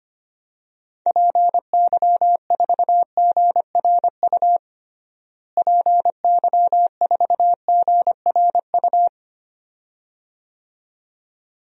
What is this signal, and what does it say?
Morse code, heard as a keyed tone, about 25 wpm, 700 Hz. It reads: PY4GRU PY4GRU
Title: DXCC-100 Call Signs - 6 Characters - Repeated 2x - 25wpm